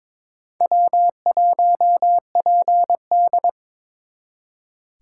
Morse code, heard as a keyed tone, about 26 wpm, 700 Hz. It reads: W1PD
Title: Speed Racing - USA 1x2 Call Signs 22wpm